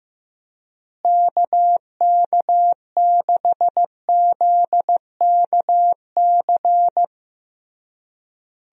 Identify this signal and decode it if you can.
Morse code, heard as a keyed tone, about 15 wpm, 700 Hz. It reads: KK6ZKC
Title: USA Call Signs - 6 Characters - Repeated 3x - 15wpm